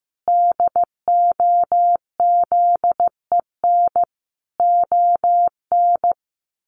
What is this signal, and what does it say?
Morse code, heard as a keyed tone, about 15 wpm, 700 Hz. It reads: DOZEN ON
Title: A Sound Of Thunder 15wpm